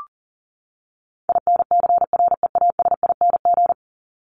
Morse code, heard as a keyed tone, about 40 wpm, 700 Hz. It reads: INCREASING
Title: ING Suffix - Top 100 Words with Base Words 40wpm